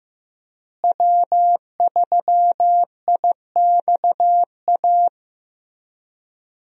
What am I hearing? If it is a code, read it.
Morse code, heard as a keyed tone, about 15 wpm, 700 Hz. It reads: W3IXA
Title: USA Call Signs - 5 Characters - Repeated 3x - 15wpm